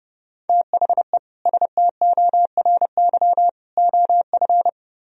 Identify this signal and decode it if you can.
Morse code, heard as a keyed tone, about 35 wpm, 700 Hz. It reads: THE STORY OF
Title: Speed-Racing - Sets of 3 Words 30wpm